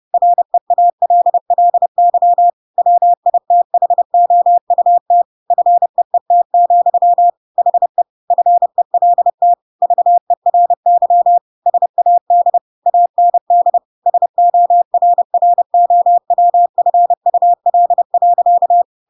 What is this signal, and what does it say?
Morse code, heard as a keyed tone, about 30 wpm, 700 Hz. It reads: REALLY WITHOUT FEET, HE FELT VERY SAD AND SORROWFUL.
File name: Pinocchio / Pinocchio ch7 30 30 ik1hge